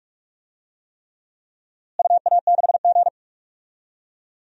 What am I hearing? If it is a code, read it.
Morse code, heard as a keyed tone, about 46 wpm, 700 Hz. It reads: UA6G
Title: DXCC-100 Call Signs - 4 Characters - Repeated 3x - 45wpm